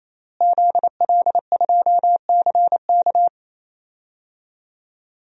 Morse code, heard as a keyed tone, about 28 wpm, 700 Hz. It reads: ZL2CK
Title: DXCC-100 Call Signs - 5 Characters - Repeated 3x - 28wpm